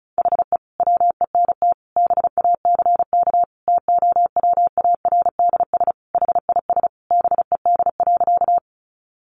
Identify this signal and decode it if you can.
Morse code, heard as a keyed tone, about 34 wpm, 700 Hz. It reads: HE WENT BACK TOWARDS HIS BED.
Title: Sentences from Top 1100 Words 35wpm